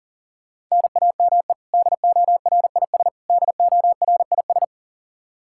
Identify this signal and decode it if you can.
Morse code, heard as a keyed tone, about 40 wpm, 700 Hz. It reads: NAME DORIS DORIS
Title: Practice QSO - Exchange 1 - Beginning - 40wpm